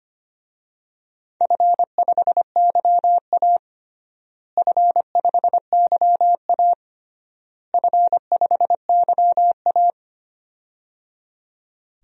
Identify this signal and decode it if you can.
Morse code, heard as a keyed tone, about 25 wpm, 700 Hz. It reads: F5YA F5YA F5YA
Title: DXCC-100 Call Signs - 4 Characters - Repeated 3x - 25wpm